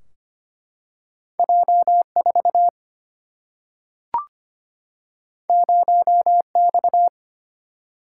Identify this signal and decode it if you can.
Morse code, heard as a keyed tone, about 25 wpm, 700 Hz. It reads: J 4 0 X
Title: Single-Letter Single-Number 25wpm